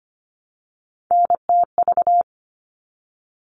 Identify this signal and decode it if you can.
Morse code, heard as a keyed tone, about 25 wpm, 700 Hz. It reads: NTV
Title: All Cut Numbers - 3-Digit 25wpm